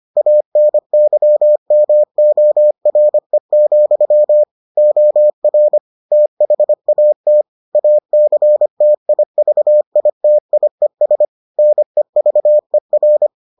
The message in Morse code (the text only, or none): ANYMORE, OR THAT ACTIVITIES NEVER